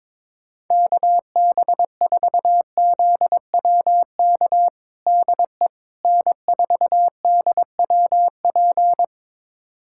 Morse code, heard as a keyed tone, about 22 wpm, 700 Hz. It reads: KB4ZWK DE N4DWP
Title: USA Callsign DE Callsign 22wpm